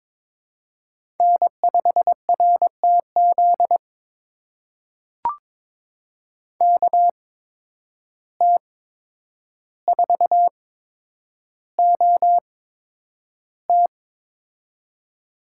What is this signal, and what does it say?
Morse code, heard as a keyed tone, about 22 wpm, spaced as for 9 wpm, 700 Hz. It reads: N5RTZ KT4OT
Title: USA Call Signs - 5 Characters - 8x Character Spacing - 22wpm